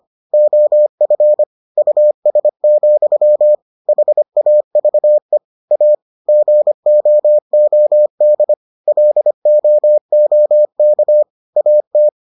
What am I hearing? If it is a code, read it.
Morse code, heard as a keyed tone, about 25 wpm, 600 Hz. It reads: OF US, HAVE A GOOD LOOK AT